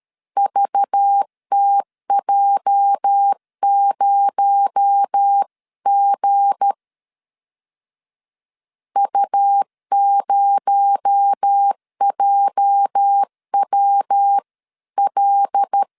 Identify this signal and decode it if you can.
Morse code, heard as a keyed tone, about 13 wpm, 800 Hz. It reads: VTJ0G U0JWL
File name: LEZIONI / 8 YJ45 / 8 YJ 12wpm 2235